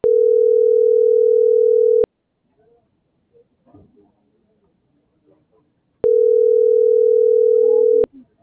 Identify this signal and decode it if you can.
Ringback tone (North America), 440 + 480 Hz.
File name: CENTRAL TERRESTRE / out karis - .76905